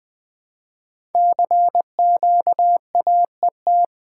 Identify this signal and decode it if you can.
Morse code, heard as a keyed tone, about 20 wpm, 700 Hz. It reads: CQAET
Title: Code Groups - 5 Letter:Numbers - 20wpm